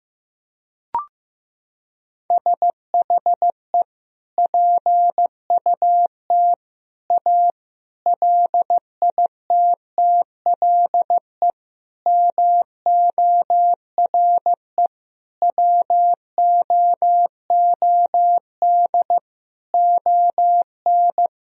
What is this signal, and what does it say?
Morse code, heard as a keyed tone, about 15 wpm, 700 Hz. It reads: SHE PUT A LITTLE MORE WOOD ON
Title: Sentences from Top 1000 Words - Review Part 2 - 15wpm